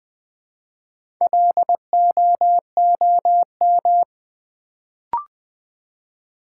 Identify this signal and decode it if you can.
Morse code, heard as a keyed tone, about 20 wpm, 700 Hz. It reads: LOOM
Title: Build A Word - 1 Letter At A Time - 3 to 5-Letter Words - 20wpm